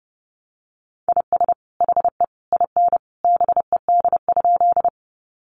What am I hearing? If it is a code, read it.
Morse code, heard as a keyed tone, about 30 wpm, 700 Hz. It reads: IS HE IN BED?
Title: Sentences from Top 500 Words 30wpm